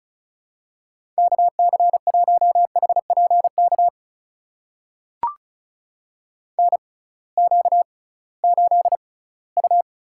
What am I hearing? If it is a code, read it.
Morse code, heard as a keyed tone, about 36 wpm, spaced as for 20 wpm, 700 Hz. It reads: KC1HPK NQ8U
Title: USA Call Signs - 6x Character Spacing - 35wpm